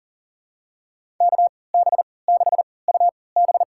Morse code, heard as a keyed tone, about 40 wpm, spaced as for 23 wpm, 700 Hz. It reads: KB6UB
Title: USA Call Signs - 3x Character Spacing - 40wpm